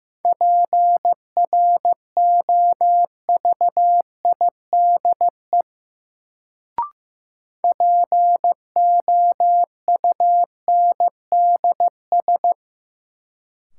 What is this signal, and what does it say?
Morse code, heard as a keyed tone, about 15 wpm, 700 Hz. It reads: PROVIDE POUNDS
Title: Sentences from Top 1000 Words - Review Part 2 - New Word Drill - 15wpm